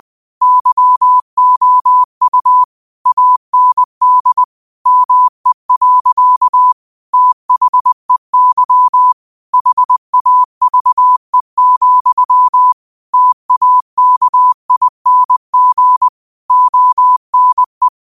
Morse code, heard as a keyed tone, about 20 wpm, 1000 Hz. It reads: YOU AND ME. THEY HAVE, TAKING ONE
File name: qotdfiles / 20wpm